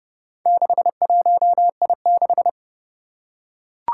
Morse code, heard as a keyed tone, about 32 wpm, 700 Hz. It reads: 61I6
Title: Four Challenging Letter-Number Combinations 30wpm